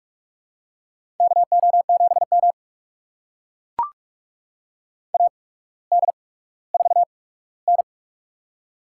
Morse code, heard as a keyed tone, about 44 wpm, spaced as for 19 wpm, 700 Hz. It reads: KO7M AD4N
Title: USA Call Signs - 4 Characters - 8x Character Spacing - 45wpm